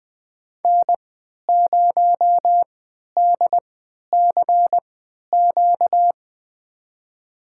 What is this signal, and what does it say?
Morse code, heard as a keyed tone, about 21 wpm, spaced as for 12 wpm, 700 Hz. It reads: N0DCQ
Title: USA Call Signs - 3x Character Spacing - 20wpm